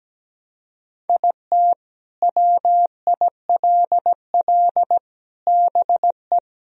Morse code, heard as a keyed tone, about 17 wpm, 700 Hz. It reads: IT WILL BE
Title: Sentences from Top 1000 Words - Review Part 2 - 17wpm